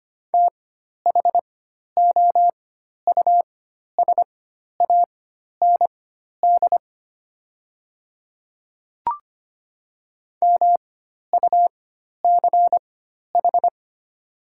Morse code, heard as a keyed tone, about 25 wpm, spaced as for 12 wpm, 690 Hz. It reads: THOUSAND MUCH
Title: 4x Character Spacing - Top 500 Words - 25wpm